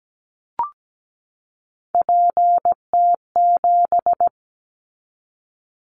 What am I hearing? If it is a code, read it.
Morse code, heard as a keyed tone, about 17 wpm, 700 Hz. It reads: PT7
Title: Callsign Prefixes for Top 50 DXCC - 17wpm